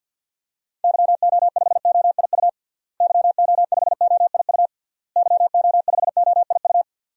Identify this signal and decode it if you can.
Morse code, heard as a keyed tone, about 50 wpm, 700 Hz. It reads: YO5OIU YO5OIU YO5OIU
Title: DXCC-100 Call Signs - 6 Characters - Repeated 3x - 50wpm